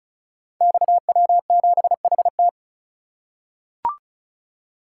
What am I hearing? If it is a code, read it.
Morse code, heard as a keyed tone, about 36 wpm, 700 Hz. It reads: XW7HT
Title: Code Groups - 5 Letter:Numbers - 35wpm